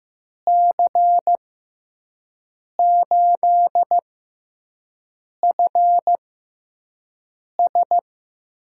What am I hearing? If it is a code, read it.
Morse code, heard as a keyed tone, about 15 wpm, spaced as for 5 wpm, 700 Hz. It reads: C8FS
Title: USA Call Signs - 6 Characters - 6x Character Spacing - 15wpm